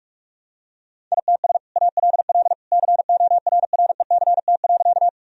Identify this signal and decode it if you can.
Morse code, heard as a keyed tone, about 46 wpm, 700 Hz. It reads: ITS ALL CORRECT.
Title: Sentences from Top 1000 Words - Review Part 2 - 45wpm